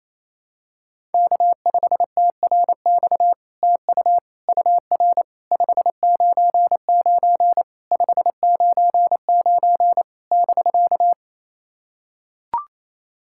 Morse code, heard as a keyed tone, about 28 wpm, 700 Hz. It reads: K5TRX TU UR 599 599 <BK>
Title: SOTA QSO Practice 28wpm